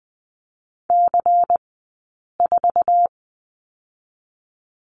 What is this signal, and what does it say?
Morse code, heard as a keyed tone, about 20 wpm, 700 Hz. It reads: C 4
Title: 158 Lesson 26d - Letter g - Two Characters 20wpm